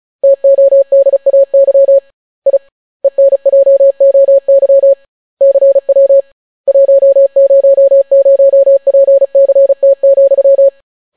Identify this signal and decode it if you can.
Morse code, heard as a keyed tone, about 36 wpm, 550 Hz. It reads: TODAY I ENJOY CW 100PCT,